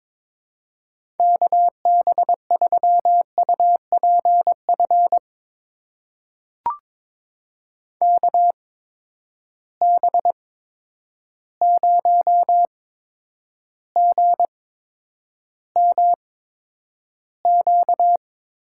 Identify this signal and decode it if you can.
Morse code, heard as a keyed tone, about 22 wpm, spaced as for 9 wpm, 700 Hz. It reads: KB3UPF KB0GMQ
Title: USA Call Signs - 6 Characters - 8x Character Spacing - 22wpm